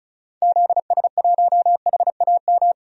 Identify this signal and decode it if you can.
Morse code, heard as a keyed tone, about 34 wpm, 700 Hz. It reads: ZS1HAM
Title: DXCC-100 Call Signs - 6 Characters - 35wpm